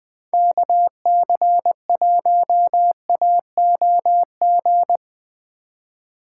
Morse code, heard as a keyed tone, about 19 wpm, 700 Hz. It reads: KC1AOG